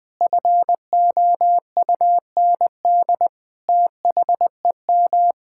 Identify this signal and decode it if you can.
Morse code, heard as a keyed tone, about 20 wpm, 700 Hz. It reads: FOUND THEM
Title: Sentences from Top 1000 Words - Review Part 6 - 20wpm